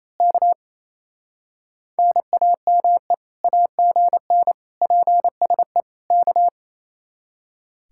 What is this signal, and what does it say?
Morse code, heard as a keyed tone, about 27 wpm, 700 Hz. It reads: K NAME AGN PSE K
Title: Speed-Racing Common QSO Phrases - 28wpm